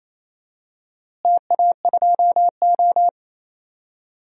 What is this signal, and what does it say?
Morse code, heard as a keyed tone, about 28 wpm, 700 Hz. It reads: TA2O
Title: DXCC-100 Call Signs - 4 Characters - Repeated 3x - 28wpm